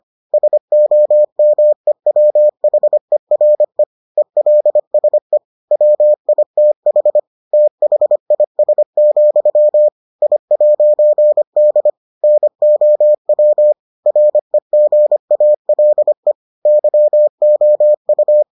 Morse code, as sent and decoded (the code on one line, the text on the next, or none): ... --- -- . .-- .... .-. / . .-.. ... / .-- .. - .... / - .... .. ... --..-- / .. .----. -.. / -. --- .-- / .-. --. .- .-.. / -.-- --- ..-
SOMEWHERE ELSE WITH THIS, I'D NOW REGALE YOU